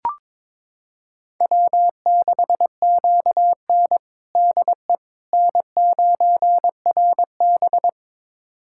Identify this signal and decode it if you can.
Morse code, heard as a keyed tone, about 22 wpm, 700 Hz. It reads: W6QN DE N9RB
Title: USA Callsign DE Callsign - 4 Characters - 22wpm